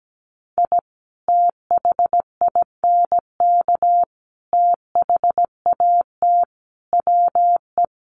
Morse code, heard as a keyed tone, about 17 wpm, 700 Hz. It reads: I THINK THAT WE
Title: Sets of 5 Words Encore2 17wpm